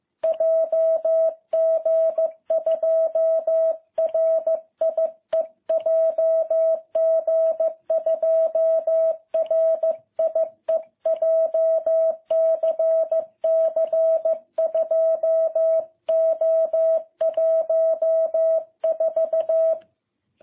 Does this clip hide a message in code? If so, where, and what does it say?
Yes: Morse code from 0.2 to 19.7 s, heard as a keyed tone, about 15 wpm, 660 Hz. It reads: JG2RIE JG2RIE JCC 2O14